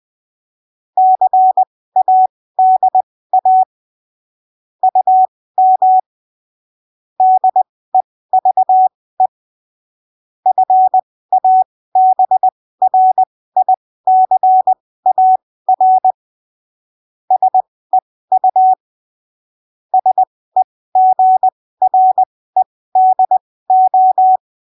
Morse code, heard as a keyed tone, about 20 wpm, spaced as for 15 wpm, 750 Hz. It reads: CADA UM DEVE FABRICAR SEU SEGREDO
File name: arquivos / texto / texto 20 20ppm